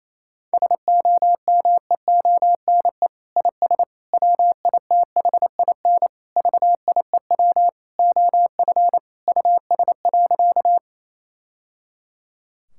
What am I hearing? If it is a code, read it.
Morse code, heard as a keyed tone, about 28 wpm, 700 Hz. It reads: SOMEONE IS WITHIN VIEW OF US.
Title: Sentences from Top 1000 Words - Review Part 6 - 28wpm